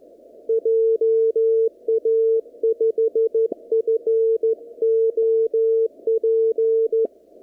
Morse code, heard as a keyed tone, about 14 wpm, 450 Hz. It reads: JA5FOP
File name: CW / kt5j wpx cw-op-k5pi